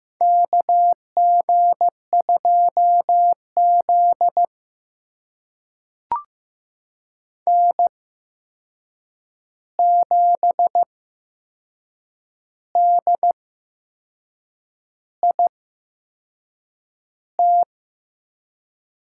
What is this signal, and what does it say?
Morse code, heard as a keyed tone, about 15 wpm, spaced as for 6 wpm, 700 Hz. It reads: KG2Z N7DIT
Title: USA Call Signs - 8x Character Spacing - 15wpm